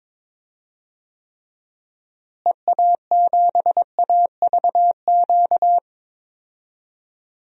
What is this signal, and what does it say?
Morse code, heard as a keyed tone, about 21 wpm, 700 Hz. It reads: EA7AVQ
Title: DXCC-100 Call Signs - 6 Characters - Repeated 3x - 22wpm